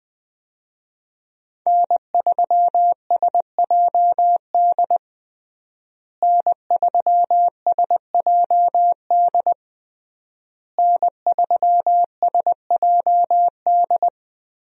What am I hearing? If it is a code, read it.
Morse code, heard as a keyed tone, about 20 wpm, 700 Hz. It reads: N3SJD N3SJD N3SJD
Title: USA Call Signs - 5 Characters - Repeated 3x - 20wpm